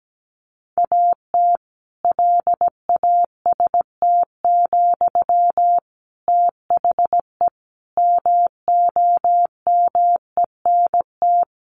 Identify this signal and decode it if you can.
Morse code, heard as a keyed tone, about 17 wpm, 700 Hz. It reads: AT LAST, THE MOMENT
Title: Sentences from Top 1500 Words 17wpm